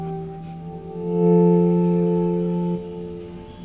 Dial tone (North America), 350 + 440 Hz.